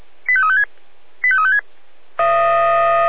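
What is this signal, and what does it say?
Five-tone selective call (ZVEI), heard as quick runs of five tones, tones of about 70 ms. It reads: 85356 85356
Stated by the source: ZVEI five-tone call received by SDR.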